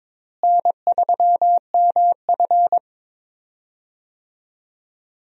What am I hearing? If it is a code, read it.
Morse code, heard as a keyed tone, about 22 wpm, 700 Hz. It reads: N3MF
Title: USA Call Signs - 4 Characters - Repeated 3x - 22wpm